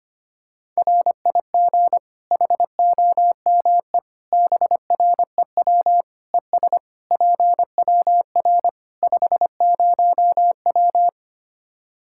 Morse code, heard as a keyed tone, about 25 wpm, 700 Hz. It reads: RIG HOME BREW ES PWR 50W
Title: K1USN SST QSO Practice 25wpm